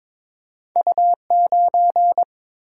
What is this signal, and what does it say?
Morse code, heard as a keyed tone, about 23 wpm, 700 Hz. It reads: U9
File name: Two Letter-Number Combinations - 22wpm